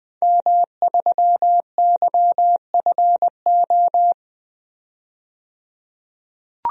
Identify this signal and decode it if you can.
Morse code, heard as a keyed tone, about 20 wpm, 700 Hz. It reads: M3YFO
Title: DXCC-100 Call Signs - 5 Characters - 20wpm